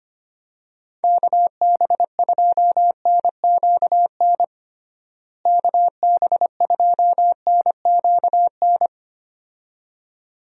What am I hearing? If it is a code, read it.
Morse code, heard as a keyed tone, about 25 wpm, 700 Hz. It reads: KB2NQN KB2NQN
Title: USA Call Signs - 6 Characters - Repeated 2x - 25wpm